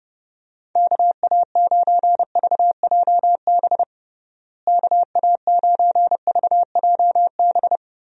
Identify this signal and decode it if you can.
Morse code, heard as a keyed tone, about 32 wpm, 700 Hz. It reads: KA9VJB KA9VJB
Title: USA Call Signs - 6 Characters - Repeated 2x - 30wpm